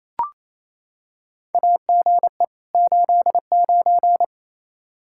Morse code, heard as a keyed tone, about 28 wpm, 700 Hz. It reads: AGE 89
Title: K1USN SST QSO Practice 28wpm